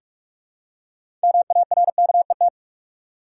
Morse code, heard as a keyed tone, about 45 wpm, 700 Hz. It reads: MARKET
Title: Six Letter Words - 45wpm